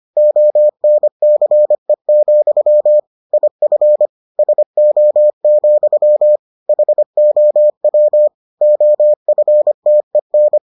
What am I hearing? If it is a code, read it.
Morse code, heard as a keyed tone, about 25 wpm, 600 Hz. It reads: ONCE, IF SO, HOW OFTEN